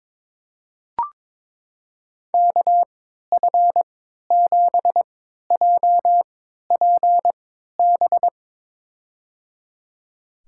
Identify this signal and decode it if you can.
Morse code, heard as a keyed tone, about 22 wpm, spaced as for 12 wpm, 700 Hz. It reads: KF7JPB
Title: USA Call Signs - 6 Characters - 3x Character Spacing - 22wpm